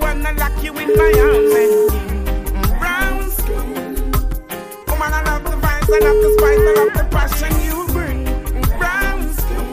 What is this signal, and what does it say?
Ringback tone (Europe, 425 Hz), with noise.